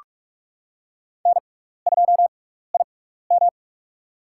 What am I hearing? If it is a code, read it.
Morse code, heard as a keyed tone, about 44 wpm, spaced as for 15 wpm, 700 Hz. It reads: N2IM
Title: USA Call Signs - 4 Characters - 6x Character Spacing - 45wpm